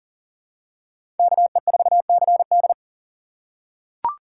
Morse code, heard as a keyed tone, about 40 wpm, 700 Hz. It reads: KE4CD
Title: USA Call Signs - 5 Characters - 4x Character Spacing - 40wpm